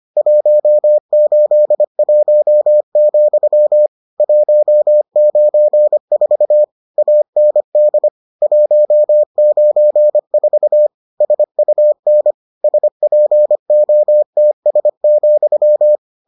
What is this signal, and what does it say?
Morse code, heard as a keyed tone, about 25 wpm, 600 Hz. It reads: 181, 194 AND 194 SUN SPOTS,